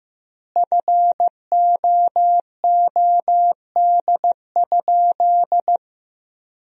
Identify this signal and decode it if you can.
Morse code, heard as a keyed tone, about 15 wpm, 700 Hz. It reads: FOOD?
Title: 2x Word-Spacing - Sentences from Top 300 Words - 15wpm